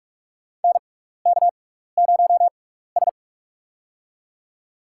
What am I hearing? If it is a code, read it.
Morse code, heard as a keyed tone, about 46 wpm, spaced as for 16 wpm, 700 Hz. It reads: NK0S
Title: USA Call Signs - 4 Characters - 6x Character Spacing - 45wpm